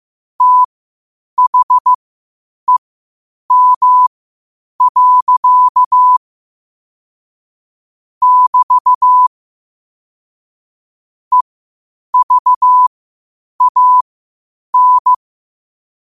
Morse code, heard as a keyed tone, about 15 wpm, spaced as for 8 wpm, 1000 Hz. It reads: THEM. <BT> EVAN
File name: qotdfiles / 8wpm